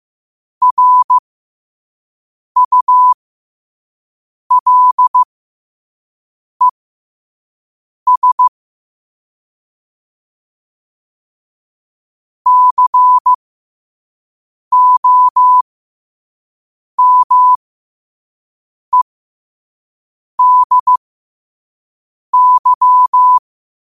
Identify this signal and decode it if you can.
Morse code, heard as a keyed tone, about 15 wpm, spaced as for 5 wpm, 1000 Hz. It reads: RULES COMEDY
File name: qotdfiles / 5wpm